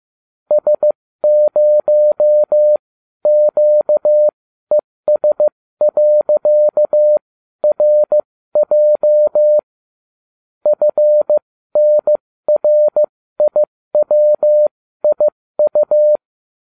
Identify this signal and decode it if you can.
Morse code, heard as a keyed tone, about 15 wpm, spaced as for 13 wpm, 600 Hz. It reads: S0QES.RJ FNRIWIU